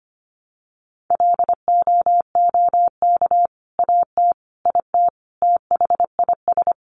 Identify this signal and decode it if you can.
Morse code, heard as a keyed tone, about 25 wpm, 700 Hz. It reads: LOOK AT IT THIS
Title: Sentences from Top 100 Words 25wpm